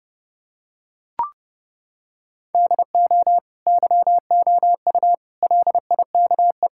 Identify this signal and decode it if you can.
Morse code, heard as a keyed tone, about 32 wpm, 700 Hz. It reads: DO YOU LIKE
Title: Sentences from Top 1000 Words - Review Part 6 - 30wpm